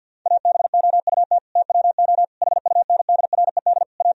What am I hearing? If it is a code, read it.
Morse code, heard as a keyed tone, about 50 wpm, 700 Hz. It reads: ABOUT TWO HUNDRED A
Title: Sentences from Top 1000 Words - Review Part 2 - 50wpm